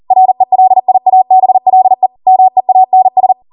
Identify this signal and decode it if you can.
Morse code, heard as a keyed tone, about 40 wpm, 750 Hz. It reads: RELIABLE MEANS